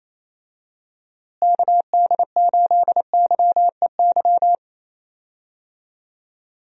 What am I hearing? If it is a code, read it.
Morse code, heard as a keyed tone, about 28 wpm, 700 Hz. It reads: KD8YEY
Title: USA Call Signs - 6 Characters - Repeated 2x - 28wpm